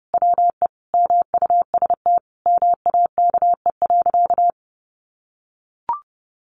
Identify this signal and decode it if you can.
Morse code, heard as a keyed tone, about 30 wpm, 700 Hz. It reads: WE MUST MAKE.
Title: Sentences from Top 1000 Words 30wpm